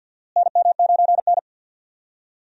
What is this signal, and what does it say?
Morse code, heard as a keyed tone, about 49 wpm, 700 Hz. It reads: NM9N
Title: USA Call Signs - 4 Characters - Repeated 2x - 50wpm